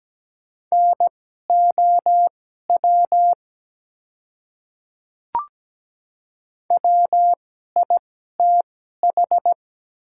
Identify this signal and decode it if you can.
Morse code, heard as a keyed tone, about 17 wpm, spaced as for 12 wpm, 700 Hz. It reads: NOW WITH
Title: 2x Character Spacing - Top 100 Words - 17wpm